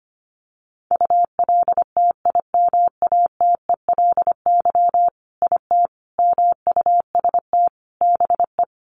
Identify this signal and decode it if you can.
Morse code, heard as a keyed tone, about 25 wpm, 700 Hz. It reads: ULTIMATELY IT MUST BE
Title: Sentences from Top 1200 Words 25wpm